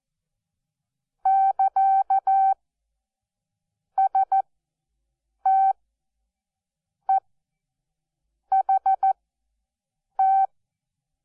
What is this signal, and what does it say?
Morse code, heard as a keyed tone, about 14 wpm, spaced as for 6 wpm, 770 Hz. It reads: <KA>STEHT
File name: Onlinelehrgang / morsen / DJ4UF Morsen12